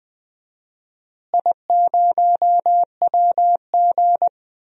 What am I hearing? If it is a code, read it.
Morse code, heard as a keyed tone, about 21 wpm, 700 Hz. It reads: I0WG
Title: DXCC-100 Call Signs - 4 Characters - Repeated 2x - 20wpm